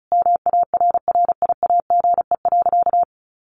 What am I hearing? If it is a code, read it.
Morse code, heard as a keyed tone, about 34 wpm, 700 Hz. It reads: MARRIAGE.
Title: Sentences from Top 1500 Words 35wpm